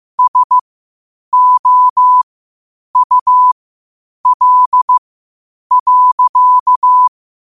Morse code, heard as a keyed tone, about 15 wpm, spaced as for 8 wpm, 1000 Hz. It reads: SOUL.
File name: qotdfiles / 8wpm